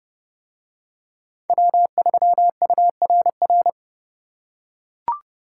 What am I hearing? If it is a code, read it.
Morse code, heard as a keyed tone, about 29 wpm, 700 Hz. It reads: W3URR